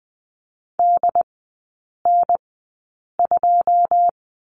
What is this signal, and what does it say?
Morse code, heard as a keyed tone, about 20 wpm, spaced as for 8 wpm, 690 Hz. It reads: DN2
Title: 081 Lesson 14e - Number 2 - Three Characters 20wpm